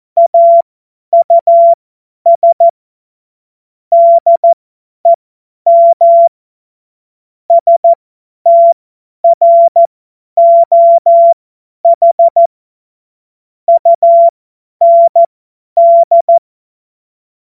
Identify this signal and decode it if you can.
Morse code, heard as a keyed tone, about 14 wpm, spaced as for 10 wpm, 680 Hz. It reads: AUS DEM STROH UND
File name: Morsekurs / Witze / Witz 02. November 2021, 50 F70